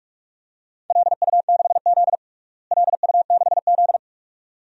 Morse code, heard as a keyed tone, about 46 wpm, 700 Hz. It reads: LU6Z LU6Z
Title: DXCC-100 Call Signs - 4 Characters - Repeated 2x - 45wpm